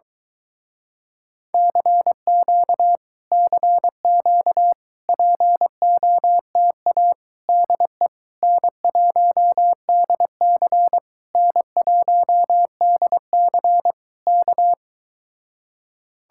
Morse code, heard as a keyed tone, about 23 wpm, 700 Hz. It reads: CQ CQ POTA DE N1DC N1DC K